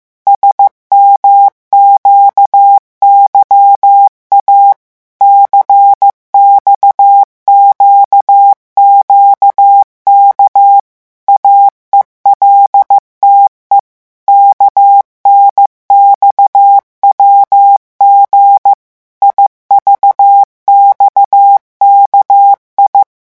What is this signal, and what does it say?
Morse code, heard as a keyed tone, about 15 wpm, 800 Hz. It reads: SMQYA CXQQK AELTE KNXWG IVXKI
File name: morse mp3 / revision7-vitesse 14 mots